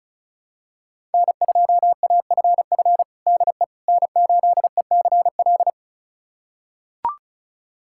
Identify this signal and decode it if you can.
Morse code, heard as a keyed tone, about 36 wpm, 700 Hz. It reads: N2AFF DE N8ECL
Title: USA Callsign DE Callsign - 5 Characters - 35wpm